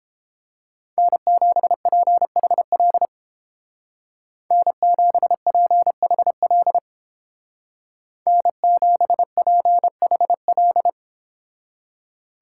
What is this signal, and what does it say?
Morse code, heard as a keyed tone, about 29 wpm, 700 Hz. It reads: N7PHL N7PHL N7PHL